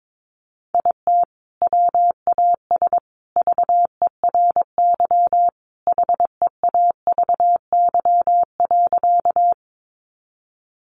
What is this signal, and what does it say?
Morse code, heard as a keyed tone, about 22 wpm, 700 Hz. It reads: IT WAS VERY HEAVY.
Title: Sentences from Top 1100 Words 22wpm